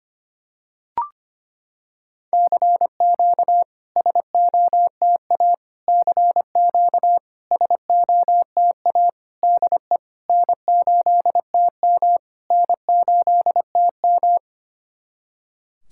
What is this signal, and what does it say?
Morse code, heard as a keyed tone, about 25 wpm, 690 Hz. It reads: CQ SOTA CQ SOTA DE N8TM N8TM
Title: SOTA QSO Practice 25wpm